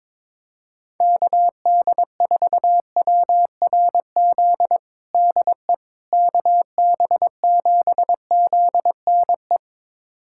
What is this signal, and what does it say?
Morse code, heard as a keyed tone, about 22 wpm, 700 Hz. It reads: KD4WRZ DE KB7ZNE
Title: USA Callsign DE Callsign - 6 Characters - 22wpm